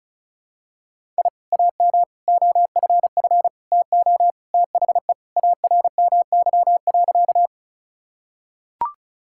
Morse code, heard as a keyed tone, about 34 wpm, 700 Hz. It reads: I AM OFF TO THE ARMY.
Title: Sentences from Top 1000 Words - Review Part 5 - 35wpm